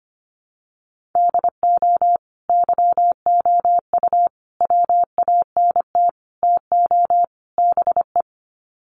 Morse code, heard as a keyed tone, about 25 wpm, 700 Hz. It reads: DO YOU WANT TO BE
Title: Sets of 5 Words Encore2 25wpm